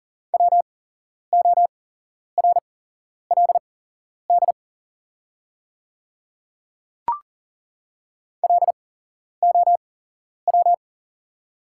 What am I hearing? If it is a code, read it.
Morse code, heard as a keyed tone, about 40 wpm, spaced as for 11 wpm, 690 Hz. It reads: WORLD LOW
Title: 8x Character Spacing - Top 300 Words - 40wpm